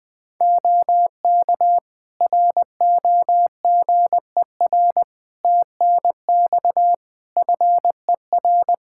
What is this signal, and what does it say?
Morse code, heard as a keyed tone, about 20 wpm, 700 Hz. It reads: OK ROGER TNX FER
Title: Practice QSO - Exchange 4 - Ending - 20wpm